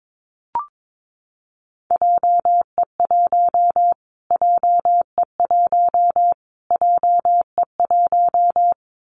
Morse code, heard as a keyed tone, about 22 wpm, 700 Hz. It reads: JE1 JE1 JE1
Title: Callsign Prefixes for Top 25 DXCC - Repeated 3x - 22wpm